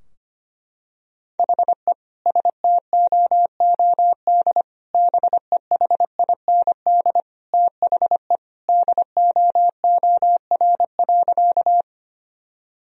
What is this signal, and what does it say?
Morse code, heard as a keyed tone, about 25 wpm, 700 Hz. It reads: HE STOOD BEHIND THE DOOR.
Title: Sentences from Top 1000 Words - Review Part 2 - 25wpm